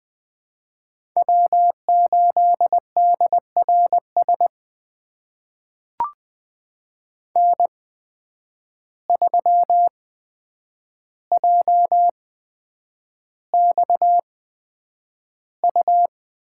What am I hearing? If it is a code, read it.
Morse code, heard as a keyed tone, about 20 wpm, spaced as for 9 wpm, 700 Hz. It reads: W8DRS N3JXU
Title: USA Call Signs - 5 Characters - 8x Character Spacing - 20wpm